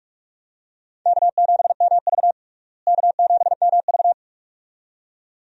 Morse code, heard as a keyed tone, about 46 wpm, 700 Hz. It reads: K7MV K7MV
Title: USA Call Signs - 4 Characters - Repeated 2x - 45wpm